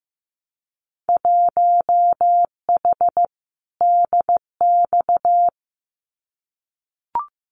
Morse code, heard as a keyed tone, about 15 wpm, 700 Hz. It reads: 1H DX
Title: Winter Field Day Exchanges 15wpm